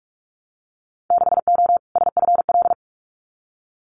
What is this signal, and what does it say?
Morse code, heard as a keyed tone, about 46 wpm, 700 Hz. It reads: 6O SFL
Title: Winter Field Day Exchanges Encore 45wpm